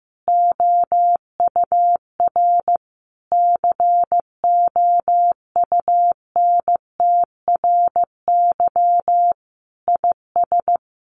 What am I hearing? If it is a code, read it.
Morse code, heard as a keyed tone, about 15 wpm, 700 Hz. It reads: OUR COUNTRY IS
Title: Sentences from Top 1200 Words 15wpm